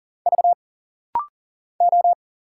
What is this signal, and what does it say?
Morse code, heard as a keyed tone, about 40 wpm, 700 Hz. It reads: V O
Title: Single Letters - Rapid-Fire 40wpm